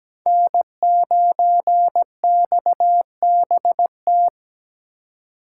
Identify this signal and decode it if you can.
Morse code, heard as a keyed tone, about 20 wpm, 700 Hz. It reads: N9XBT
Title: Speed Racing - USA 1x3 Call Signs 17wpm